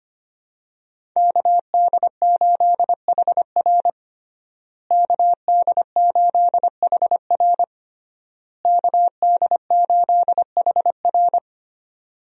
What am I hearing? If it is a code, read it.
Morse code, heard as a keyed tone, about 25 wpm, 700 Hz. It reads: KD8HR KD8HR KD8HR
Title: USA Call Signs - 5 Characters - Repeated 3x - 25wpm